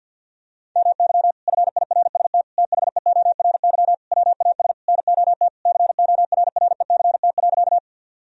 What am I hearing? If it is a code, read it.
Morse code, heard as a keyed tone, about 53 wpm, 700 Hz. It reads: MY FIRST THEORY WAS NOT CORRECT.
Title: Sentences from Top 1000 Words - Review Part 3 - 50wpm